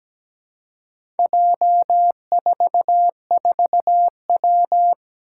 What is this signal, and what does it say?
Morse code, heard as a keyed tone, about 17 wpm, 700 Hz. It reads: J44W
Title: Four Challenging Letter-Number Combinations 17wpm